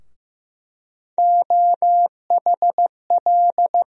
Morse code, heard as a keyed tone, about 15 wpm, 700 Hz. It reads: OHL
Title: USA Call Sign Postfixes - 3 Letters 15wpm